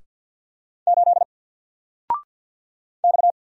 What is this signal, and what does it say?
Morse code, heard as a keyed tone, about 48 wpm, 700 Hz. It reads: 8 X
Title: Single Character (Letter:Number:Punctuation:Prosign) Alt 50wpm